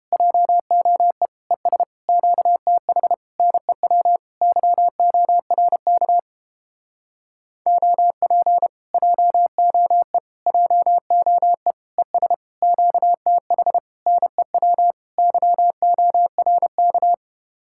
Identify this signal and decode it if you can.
Morse code, heard as a keyed tone, about 29 wpm, 700 Hz. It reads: JOE ES QTH NEW YORK OP JOE JOE ES QTH NEW YORK